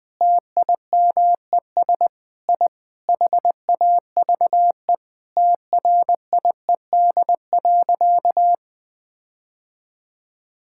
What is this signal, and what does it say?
Morse code, heard as a keyed tone, about 19 wpm, 700 Hz. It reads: TIMES I HAVE TRIED.
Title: Sentences with 7-Word Phrases - 20wpm